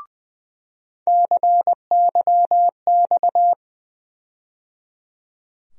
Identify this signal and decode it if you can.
Morse code, heard as a keyed tone, about 20 wpm, 700 Hz. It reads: CYX
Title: USA Call Sign Postfixes - 3 Letters 20wpm